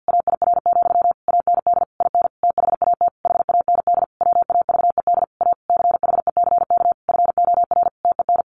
Morse code, heard as a keyed tone, about 50 wpm, 700 Hz. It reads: AIR, AND IN THAT HAND WAVED A CHECK FOR TEN
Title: A Sound Of Thunder 50wpm